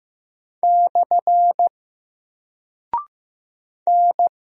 Morse code, heard as a keyed tone, about 15 wpm, 700 Hz. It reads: / N
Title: Single Character (Letter:Number:Punctuation:Prosign) 15wpm